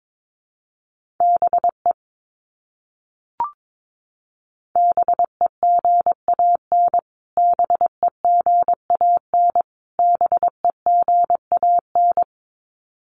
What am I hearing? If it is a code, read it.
Morse code, heard as a keyed tone, about 22 wpm, 700 Hz. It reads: BE BEGAN BEGAN BEGAN
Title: Top 500 Words Repeated 22wpm